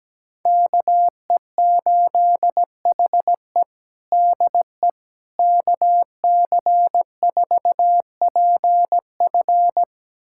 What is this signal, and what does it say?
Morse code, heard as a keyed tone, about 17 wpm, 700 Hz. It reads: KE8HE DE KC4PF
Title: USA Callsign DE Callsign - 5 Characters - 17wpm